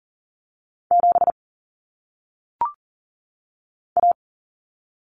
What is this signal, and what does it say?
Morse code, heard as a keyed tone, about 40 wpm, 700 Hz. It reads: 7 A
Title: Common Cut Numbers - 1-Digit 40wpm